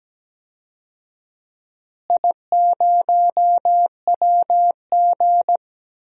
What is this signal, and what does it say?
Morse code, heard as a keyed tone, about 17 wpm, 700 Hz. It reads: I0WG
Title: DXCC-100 Call Signs - 4 Characters - Repeated 2x - 17wpm